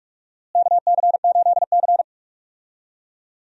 Morse code, heard as a keyed tone, about 44 wpm, 700 Hz. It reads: KC8C
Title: USA Call Signs - 4 Characters - Repeated 3x - 45wpm